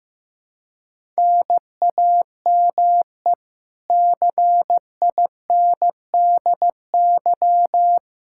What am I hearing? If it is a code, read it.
Morse code, heard as a keyed tone, about 16 wpm, 700 Hz. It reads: NAME CINDY
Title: K1USN SST QSO Practice 15wpm